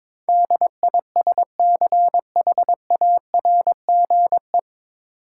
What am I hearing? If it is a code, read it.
Morse code, heard as a keyed tone, about 22 wpm, 700 Hz. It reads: DISCHARGE
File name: 9 Letter Words - First 3 Letters Given - 22wpm